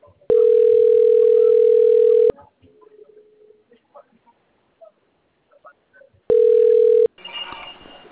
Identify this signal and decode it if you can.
Ringback tone (North America), 440 + 480 Hz.